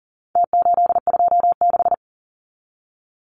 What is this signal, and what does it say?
Morse code, heard as a keyed tone, about 40 wpm, 700 Hz. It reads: T826
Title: Numbers - Contest Style - 4-Digit 40wpm